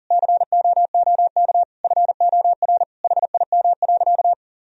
Morse code, heard as a keyed tone, about 40 wpm, 700 Hz. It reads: COOK FOR HIM.
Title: Sentences from Top 1000 Words - Review Part 4 - 40wpm